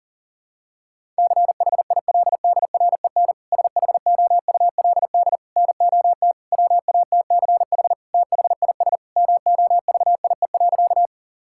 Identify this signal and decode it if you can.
Morse code, heard as a keyed tone, about 40 wpm, 700 Hz. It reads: CHILDREN SHOULD NOT WATCH THIS MOVIE.
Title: Sentences from Top 1000 Words - Review Part 5 - 40wpm